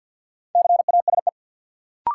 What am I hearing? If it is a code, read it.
Morse code, heard as a keyed tone, about 49 wpm, 700 Hz. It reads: CASE
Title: Sentences from Top 700 Words - New Word Drill - 50wpm